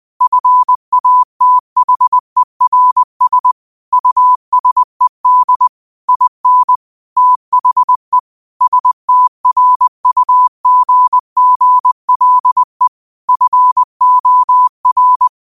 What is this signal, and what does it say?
Morse code, heard as a keyed tone, about 20 wpm, 1000 Hz. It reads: FATHERS USED IN THE STRUGGLE FOR